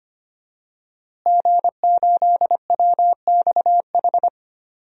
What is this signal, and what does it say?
Morse code, heard as a keyed tone, about 25 wpm, 700 Hz. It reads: G8WXH
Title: DXCC-100 Call Signs - 5 Characters - Repeated 2x - 25wpm